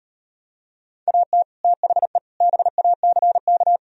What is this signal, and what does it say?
Morse code, heard as a keyed tone, about 38 wpm, 700 Hz. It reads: AT THE BACK